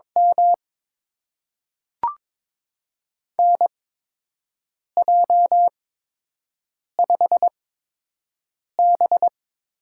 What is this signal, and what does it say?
Morse code, heard as a keyed tone, about 22 wpm, spaced as for 6 wpm, 700 Hz. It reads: M NJ5B
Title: USA Call Signs - 4 Characters - 8x Character Spacing - 22wpm